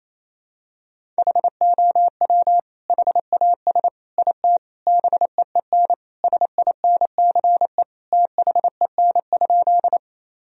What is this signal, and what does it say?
Morse code, heard as a keyed tone, about 28 wpm, 700 Hz. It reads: HOW HAS IT BEEN SINCE THEN?
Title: Sentences from Top 1000 Words - Review Part 1 - 28wpm